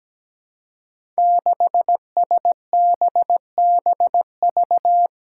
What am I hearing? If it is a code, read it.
Morse code, heard as a keyed tone, about 17 wpm, 700 Hz. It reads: 6SBBV
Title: Five Challenging Letter-Number Combinations 17wpm